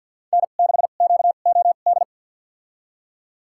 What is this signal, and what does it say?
Morse code, heard as a keyed tone, about 53 wpm, spaced as for 37 wpm, 700 Hz. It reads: N6QOD
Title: USA Call Signs - 2x Character Spacing - 50wpm